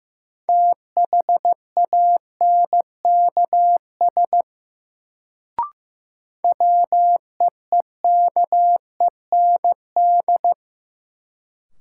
Morse code, heard as a keyed tone, about 15 wpm, 700 Hz. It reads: THANKS WEEKEND
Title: Sets of 5 Words - New Word Drill - 15wpm